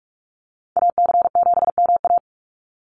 Morse code, heard as a keyed tone, about 46 wpm, 700 Hz. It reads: AC7MA
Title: NA SOTA Activator Callsigns 45wpm